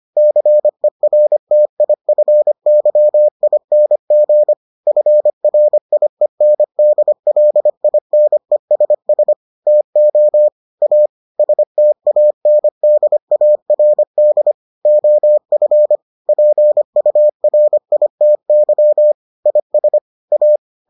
Morse code, heard as a keyed tone, about 25 wpm, 600 Hz. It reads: CERTIFYING FRIENDLINESS TO A STANDARD OF PURITY IS A